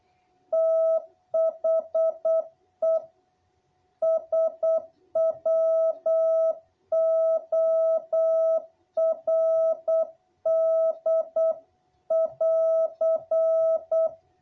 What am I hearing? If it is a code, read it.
Morse code, heard as a keyed tone, about 8 wpm, 660 Hz. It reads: THE SWORD<AR>
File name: jh8cbh / 2021cw